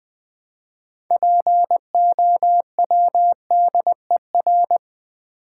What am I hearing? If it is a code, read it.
Morse code, heard as a keyed tone, about 20 wpm, 700 Hz. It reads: POWDER
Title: Build A Word - 3 Letters At A Time - 6-Letter Words - 20wpm